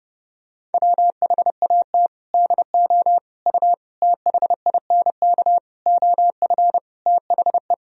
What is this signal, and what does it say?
Morse code, heard as a keyed tone, about 30 wpm, 700 Hz. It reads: WHAT DO U THINK OF THE
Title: Sentences from Top 300 Words and Abbreviations 30wpm